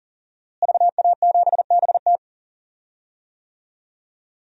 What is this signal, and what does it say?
Morse code, heard as a keyed tone, about 37 wpm, 700 Hz. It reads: VA7BT
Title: DXCC-100 Call Signs - 5 Characters - Repeated 2x - 40wpm